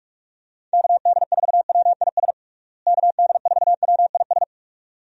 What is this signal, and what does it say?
Morse code, heard as a keyed tone, about 46 wpm, 700 Hz. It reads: KD4WIS KD4WIS
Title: USA Call Signs - 6 Characters - Repeated 2x - 45wpm